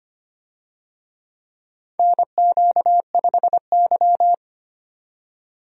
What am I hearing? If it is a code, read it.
Morse code, heard as a keyed tone, about 25 wpm, 700 Hz. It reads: NQ5Y
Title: USA Call Signs - 4 Characters - 25wpm